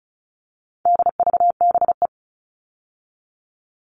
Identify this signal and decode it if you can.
Morse code, heard as a keyed tone, about 34 wpm, 700 Hz. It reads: DVBE
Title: All Cut Numbers - 4-Digit 35wpm